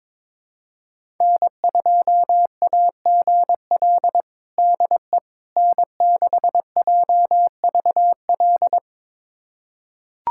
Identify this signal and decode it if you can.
Morse code, heard as a keyed tone, about 22 wpm, 700 Hz. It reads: N2AGL DE N6JVL
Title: USA Callsign DE Callsign - 5 Characters - 22wpm